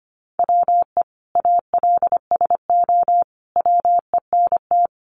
Morse code, heard as a keyed tone, about 25 wpm, 700 Hz. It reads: WE ALSO WENT
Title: Sentences from Top 1200 Words 25wpm